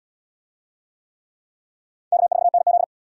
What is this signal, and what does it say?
Morse code, heard as a keyed tone, about 75 wpm, 700 Hz. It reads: K4NZ